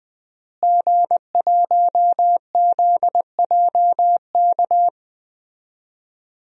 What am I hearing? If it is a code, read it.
Morse code, heard as a keyed tone, about 20 wpm, 700 Hz. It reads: G1ZJK
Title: DXCC-100 Call Signs - 5 Characters - Repeated 3x - 20wpm